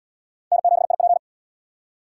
Morse code, heard as a keyed tone, about 65 wpm, 700 Hz. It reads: N7IP